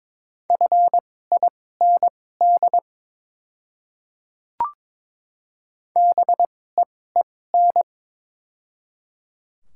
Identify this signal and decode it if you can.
Morse code, heard as a keyed tone, about 23 wpm, spaced as for 16 wpm, 700 Hz. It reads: FIND BEEN
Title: 2x Character Spacing - Top 200 Words - 22wpm